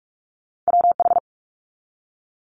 Morse code, heard as a keyed tone, about 46 wpm, 700 Hz. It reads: WH
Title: Multi-letter Phonemes 45wpm